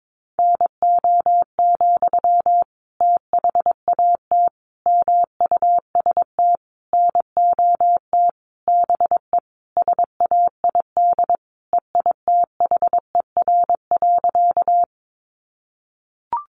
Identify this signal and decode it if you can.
Morse code, heard as a keyed tone, about 22 wpm, 700 Hz. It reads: NO, THAT MUST NOT BE SAID EITHER.
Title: Sentences from Top 600 Words 22wpm